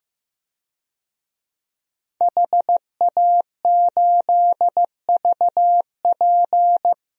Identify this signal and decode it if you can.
Morse code, heard as a keyed tone, about 16 wpm, 700 Hz. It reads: HA8VP
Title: DXCC-100 Call Signs - 5 Characters - Repeated 3x - 15wpm